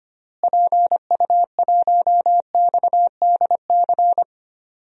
Morse code, heard as a keyed tone, about 25 wpm, 700 Hz. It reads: PU1XDC
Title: DXCC-100 Call Signs - 6 Characters - Repeated 2x - 25wpm